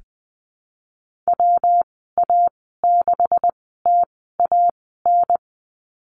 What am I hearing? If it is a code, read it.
Morse code, heard as a keyed tone, about 20 wpm, spaced as for 15 wpm, 700 Hz. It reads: WA6TAN
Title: 128 Lesson 21g - Number 6 - Call Signs 20wpm